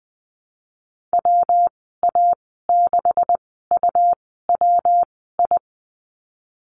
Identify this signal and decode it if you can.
Morse code, heard as a keyed tone, about 20 wpm, spaced as for 15 wpm, 700 Hz. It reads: WA6UWI
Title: 128 Lesson 21g - Number 6 - Call Signs 20wpm